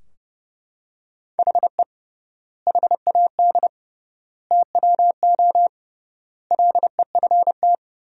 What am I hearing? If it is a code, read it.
Morse code, heard as a keyed tone, about 30 wpm, 700 Hz. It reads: HE HAD TWO LEFT
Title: 3x Word-Spacing - Sentences from Top 300 Words - 30wpm